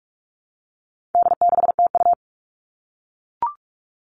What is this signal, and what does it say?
Morse code, heard as a keyed tone, about 44 wpm, 700 Hz. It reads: D6TU
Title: All Cut Numbers - 4-Digit 45wpm